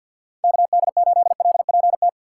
Morse code, heard as a keyed tone, about 50 wpm, 700 Hz. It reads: KD8LPT
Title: USA Call Signs - 4x Character Spacing - 50wpm